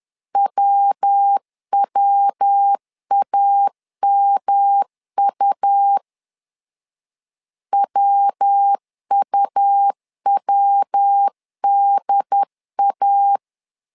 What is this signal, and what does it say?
Morse code, heard as a keyed tone, about 10 wpm, 800 Hz. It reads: WWAMU WUWDA
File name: LEZIONI / 3 MWRO / 3 MW 10wpm 1959